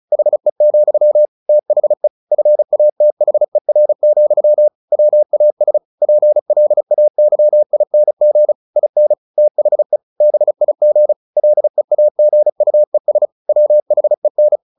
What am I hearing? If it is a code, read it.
Morse code, heard as a keyed tone, about 34 wpm, 600 Hz. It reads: HE, THE FATHER, WAS PLAYING IN THE BIG LEAGUES WHEN
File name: the old man and the sea / the old man and the sea 1 35wpm